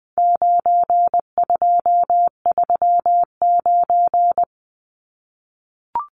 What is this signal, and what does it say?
Morse code, heard as a keyed tone, about 20 wpm, 700 Hz. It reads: 9239
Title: Numbers - 4-Digit 20wpm